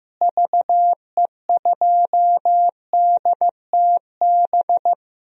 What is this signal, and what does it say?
Morse code, heard as a keyed tone, about 15 wpm, 700 Hz. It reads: VE2DTB
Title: DXCC-100 Call Signs - 6 Characters - Repeated 3x - 15wpm